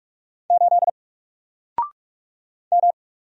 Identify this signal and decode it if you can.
Morse code, heard as a keyed tone, about 46 wpm, 700 Hz. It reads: 8 M
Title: Single Letter-Number 45wpm